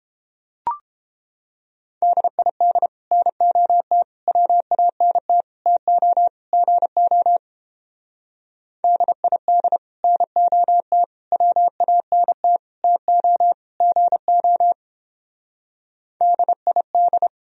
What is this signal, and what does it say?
Morse code, heard as a keyed tone, about 29 wpm, 700 Hz. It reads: DID NOT WANT TO GO DID NOT WANT TO GO DID